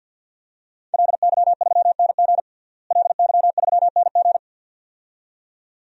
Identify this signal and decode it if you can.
Morse code, heard as a keyed tone, about 53 wpm, 700 Hz. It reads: LY3NG LY3NG
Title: DXCC-100 Call Signs - 5 Characters - Repeated 2x - 50wpm